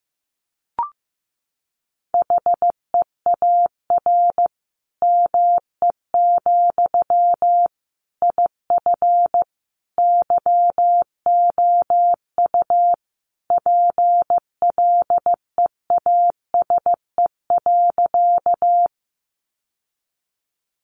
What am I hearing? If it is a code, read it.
Morse code, heard as a keyed tone, about 14 wpm, 700 Hz. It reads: HEAR ME, IF YOU PLEASE.
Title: Sentences from Top 1500 Words 15wpm